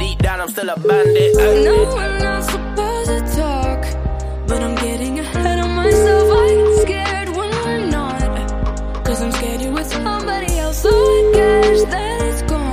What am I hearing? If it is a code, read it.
Ringback tone (Europe, 425 Hz), with noise.